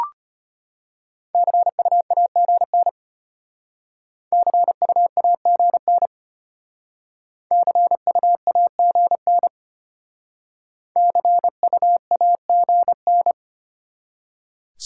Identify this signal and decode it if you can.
Morse code, heard as a keyed tone, about 32 wpm, 700 Hz. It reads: CU AGN CU AGN CU AGN CU AGN